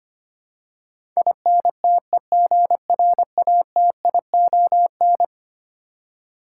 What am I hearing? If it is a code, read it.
Morse code, heard as a keyed tone, about 25 wpm, 700 Hz. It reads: INTEGRATION
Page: morse-code-ninja-practice-25wpm